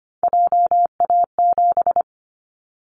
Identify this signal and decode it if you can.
Morse code, heard as a keyed tone, about 25 wpm, 700 Hz. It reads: JA7
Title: Callsign Prefixes for Top 50 DXCC - 25wpm